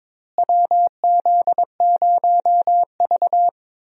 Morse code, heard as a keyed tone, about 22 wpm, 700 Hz. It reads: WZ0V
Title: USA Call Signs - 4 Characters - Repeated 3x - 22wpm